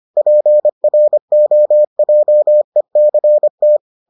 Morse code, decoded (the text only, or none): PROJECT